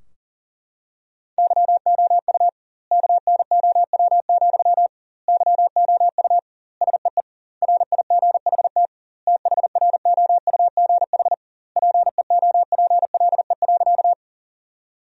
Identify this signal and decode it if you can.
Morse code, heard as a keyed tone, about 40 wpm, 700 Hz. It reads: YOU KNOW, YOU SEE RIGHT THROUGH PEOPLE.
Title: 2x Word-Spacing - Sentences from Top 200 Words - 40wpm